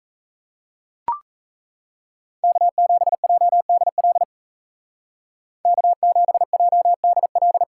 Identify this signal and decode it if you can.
Morse code, heard as a keyed tone, about 37 wpm, 700 Hz. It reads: K7JDL K7JDL